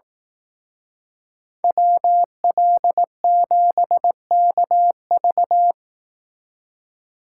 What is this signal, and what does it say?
Morse code, heard as a keyed tone, about 18 wpm, 700 Hz. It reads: WL7KV
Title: Speed Racing - USA 2x2 Call Signs 15wpm